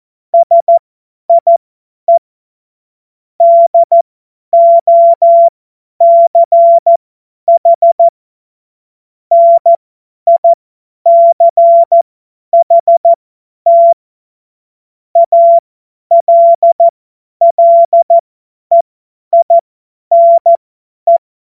Morse code, heard as a keyed tone, about 14 wpm, spaced as for 10 wpm, 680 Hz. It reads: SIE DOCH NICHT ALLEINE
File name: Witz-19-August-2025-50-F70